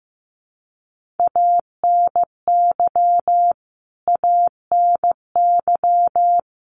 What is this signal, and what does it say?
Morse code, heard as a keyed tone, about 15 wpm, 700 Hz. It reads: ANY ANY
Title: Top 200 Words Repeated 15wpm